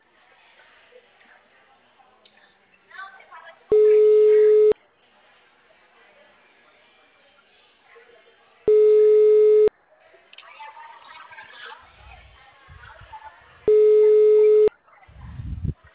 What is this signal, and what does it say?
Ringback tone (Europe, 425 Hz), with speech.